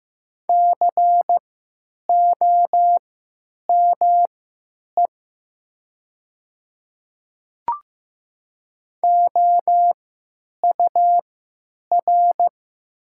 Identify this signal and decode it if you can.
Morse code, heard as a keyed tone, about 15 wpm, spaced as for 9 wpm, 700 Hz. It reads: COME OUR
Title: 3x Character Spacing - Top 500 Words - 15wpm